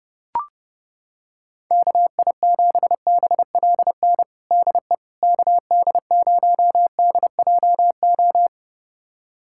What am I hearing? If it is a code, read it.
Morse code, heard as a keyed tone, about 32 wpm, 700 Hz. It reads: KI7BLN DE KD0DJO
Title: USA Callsign DE Callsign 30wpm